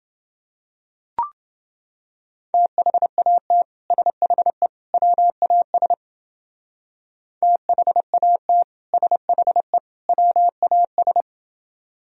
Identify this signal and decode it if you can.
Morse code, heard as a keyed tone, about 29 wpm, 700 Hz. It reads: THAT SHE WAS THAT SHE WAS